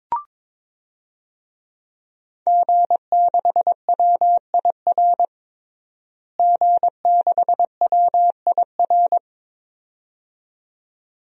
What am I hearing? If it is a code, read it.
Morse code, heard as a keyed tone, about 22 wpm, 700 Hz. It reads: G6WIR G6WIR
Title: DXCC-100 Call Signs - 5 Characters - Repeated 2x - 22wpm